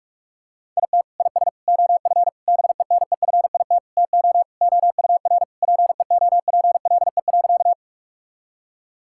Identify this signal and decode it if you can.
Morse code, heard as a keyed tone, about 45 wpm, 700 Hz. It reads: IT IS OF BENEFIT TO OUR PEOPLE.
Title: Sentences from Top 1000 Words - Review Part 6 - 45wpm